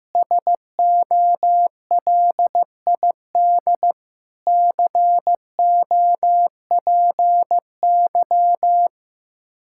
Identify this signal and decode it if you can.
Morse code, heard as a keyed tone, about 15 wpm, 700 Hz. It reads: SOLID COPY
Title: Practice QSO - Exchange 3 - More Info - 15wpm